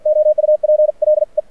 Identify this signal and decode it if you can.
Morse code, heard as a keyed tone, about 52 wpm, spaced as for 45 wpm, 590 Hz. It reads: OUPLE
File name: archived / CRSnet